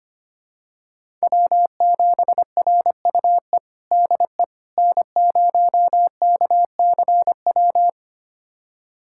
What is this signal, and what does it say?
Morse code, heard as a keyed tone, about 25 wpm, 700 Hz. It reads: W7RUE DE N0KCW
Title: USA Callsign DE Callsign - 5 Characters - 25wpm